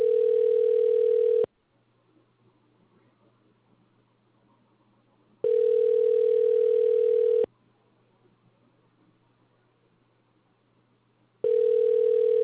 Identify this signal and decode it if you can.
Ringback tone (North America), 440 + 480 Hz.